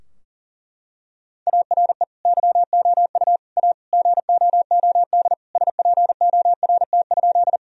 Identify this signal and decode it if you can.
Morse code, heard as a keyed tone, about 40 wpm, 700 Hz. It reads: ARE YOU A GOOD SPORT?
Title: Sentences from Top 1000 Words - Review Part 4 - 40wpm